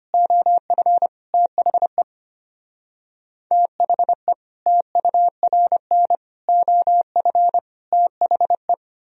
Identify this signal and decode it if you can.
Morse code, heard as a keyed tone, about 33 wpm, 700 Hz. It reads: OF THE THE TURN OF THE